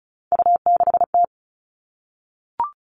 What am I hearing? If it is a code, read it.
Morse code, heard as a keyed tone, about 34 wpm, 700 Hz. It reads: U6T
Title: All Cut Numbers - 3-Digit 35wpm